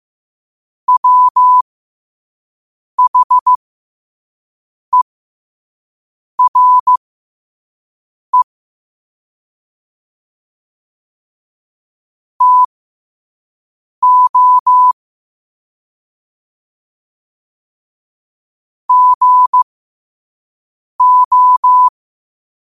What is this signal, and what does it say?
Morse code, heard as a keyed tone, about 15 wpm, spaced as for 5 wpm, 1000 Hz. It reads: WHERE TO GO
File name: qotdfiles / 5wpm